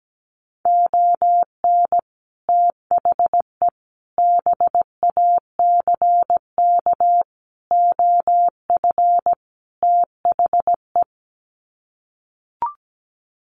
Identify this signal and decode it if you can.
Morse code, heard as a keyed tone, about 17 wpm, 700 Hz. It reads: ON THE BACK OF THE
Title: Sets of 5 Words 17wpm